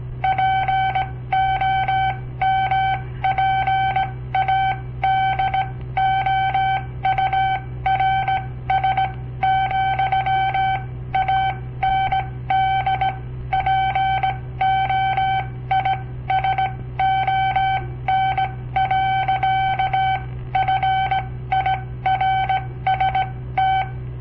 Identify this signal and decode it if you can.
Morse code, heard as a keyed tone, about 18 wpm, spaced as for 15 wpm, 770 Hz. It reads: POMPADOURS, AND POISON. FIRST
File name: papertapes / local / Code Practice-01